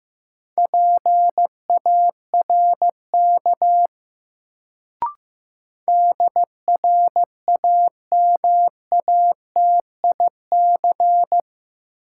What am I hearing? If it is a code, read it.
Morse code, heard as a keyed tone, about 15 wpm, 700 Hz. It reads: PARK DRAMATIC
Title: Sentences from Top 1300 Words - New Word Drill Alt - 15wpm